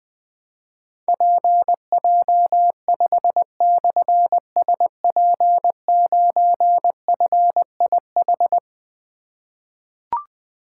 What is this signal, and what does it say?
Morse code, heard as a keyed tone, about 20 wpm, 700 Hz. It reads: PJ5/SP9FIH
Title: Difficult Call Signs 20wpm